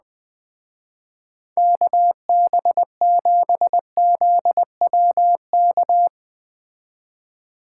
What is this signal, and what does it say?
Morse code, heard as a keyed tone, about 21 wpm, 700 Hz. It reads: KB7ZWK